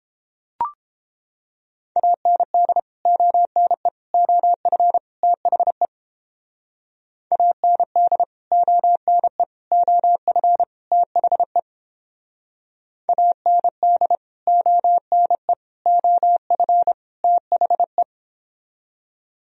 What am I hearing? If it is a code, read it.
Morse code, heard as a keyed tone, about 31 wpm, 700 Hz. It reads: AND ONE OF THE AND ONE OF THE AND ONE OF THE